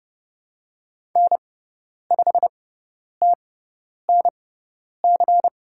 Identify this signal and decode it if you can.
Morse code, heard as a keyed tone, about 30 wpm, spaced as for 10 wpm, 700 Hz. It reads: N5TNC
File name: Lesson 16g - Letter c - Call Signs 30wpm